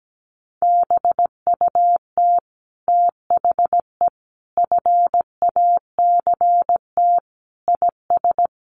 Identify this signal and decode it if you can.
Morse code, heard as a keyed tone, about 17 wpm, 700 Hz. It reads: BUT THE FACT IS
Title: Sets of 5 Words Encore2 17wpm